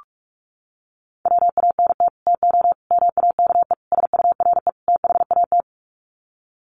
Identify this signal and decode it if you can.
Morse code, heard as a keyed tone, about 44 wpm, 700 Hz. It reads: WANT TO MAKE SURE THAT
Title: Sets of 5 Words 45wpm